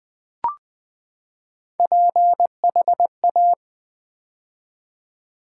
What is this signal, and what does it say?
Morse code, heard as a keyed tone, about 20 wpm, 700 Hz. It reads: PHA
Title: Build A Word - 3 Letters At A Time - 8-Letter Words - 20wpm